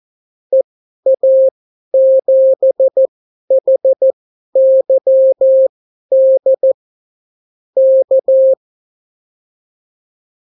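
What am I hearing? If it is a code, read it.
Morse code, heard as a keyed tone, about 14 wpm, spaced as for 11 wpm, 530 Hz. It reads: EA7HYD K
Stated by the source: Example Spanish CW QSO from a Morse learning book, 14 wpm characters with 8 wpm spacing.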